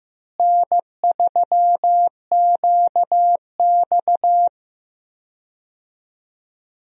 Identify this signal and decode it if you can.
Morse code, heard as a keyed tone, about 15 wpm, 700 Hz. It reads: N3QX
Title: USA Call Signs - 4 Characters - Repeated 3x - 15wpm